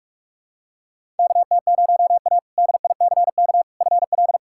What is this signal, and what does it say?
Morse code, heard as a keyed tone, about 46 wpm, 700 Hz. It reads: KT0A DICK FL
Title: K1USN SST QSO Practice - 45wpm